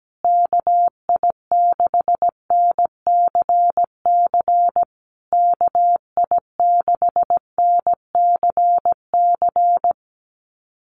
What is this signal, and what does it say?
Morse code, heard as a keyed tone, about 17 wpm, 700 Hz. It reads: KI6NCC KI6NCC
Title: USA Call Signs Repeated 3x 17wpm